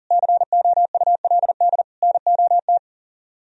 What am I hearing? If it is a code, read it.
Morse code, heard as a keyed tone, about 52 wpm, 700 Hz. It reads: COULD NOT